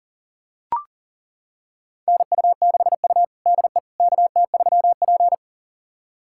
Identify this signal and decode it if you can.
Morse code, heard as a keyed tone, about 40 wpm, 700 Hz. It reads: NU6U DE KT3P
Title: USA Callsign DE Callsign - 4 Characters - 40wpm